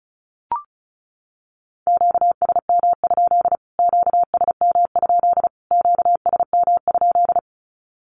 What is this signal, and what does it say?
Morse code, heard as a keyed tone, about 34 wpm, 700 Hz. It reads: QSM? QSM? QSM?
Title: Q-Codes Repeated 3x 35wpm